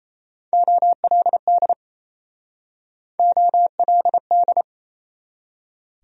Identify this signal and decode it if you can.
Morse code, heard as a keyed tone, about 37 wpm, 700 Hz. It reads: OLD OLD